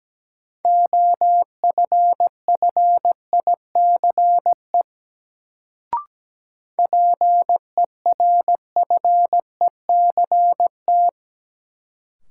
Morse code, heard as a keyed tone, about 17 wpm, 700 Hz. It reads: OFFICE PERFECT
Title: Sentences from Top 1000 Words - New Word Drill - 17wpm